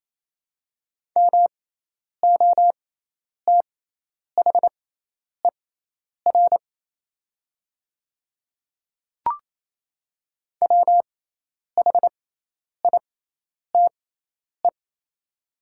Morse code, heard as a keyed tone, about 28 wpm, spaced as for 10 wpm, 700 Hz. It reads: MOTHER WHITE
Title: 6x Character Spacing - Top 500 Words - 28wpm